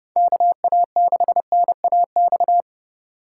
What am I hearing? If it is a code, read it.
Morse code, heard as a keyed tone, about 28 wpm, 700 Hz. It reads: KA6NAX